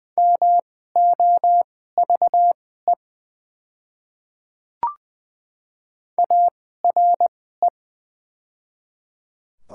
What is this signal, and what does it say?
Morse code, heard as a keyed tone, about 20 wpm, spaced as for 14 wpm, 700 Hz. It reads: MOVE ARE
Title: 2x Character Spacing - Top 200 Words - 20wpm